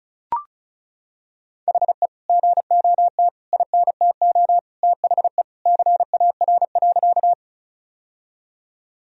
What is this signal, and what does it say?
Morse code, heard as a keyed tone, about 36 wpm, 700 Hz. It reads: HE GOT INTO THE CAR.
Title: Sentences from Top 1000 Words - Review Part 1 - 35wpm